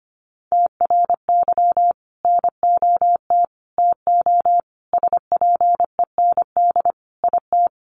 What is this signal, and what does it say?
Morse code, heard as a keyed tone, about 25 wpm, 700 Hz. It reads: TRY NOT TO SPEND IT
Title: Sentences from Top 800 Words 25wpm